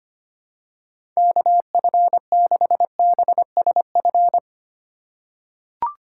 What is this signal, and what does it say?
Morse code, heard as a keyed tone, about 25 wpm, 700 Hz. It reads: KF6BSF
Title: USA Call Signs - 6 Characters - 3x Character Spacing - 25wpm